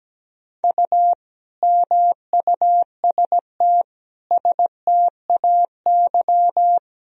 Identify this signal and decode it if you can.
Morse code, heard as a keyed tone, about 17 wpm, 700 Hz. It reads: U MUST STAY
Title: Sentences from Top 500 Words and Abbreviations 17wpm